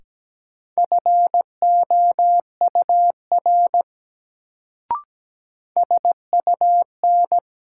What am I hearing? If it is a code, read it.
Morse code, heard as a keyed tone, about 17 wpm, 700 Hz. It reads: FOUR SUN
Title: Sentences from Top 300 Words - New Word Drill - 17wpm